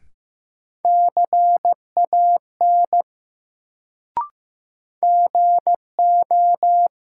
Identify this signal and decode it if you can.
Morse code, heard as a keyed tone, about 15 wpm, 700 Hz. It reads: CAN GO
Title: Sentences from Top 100 Words - New Word Drill - 15wpm